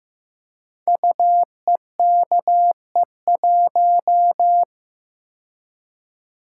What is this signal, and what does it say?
Morse code, heard as a keyed tone, about 15 wpm, 700 Hz. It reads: UEKE1
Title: Code Groups - 5 Letter:Numbers - 15wpm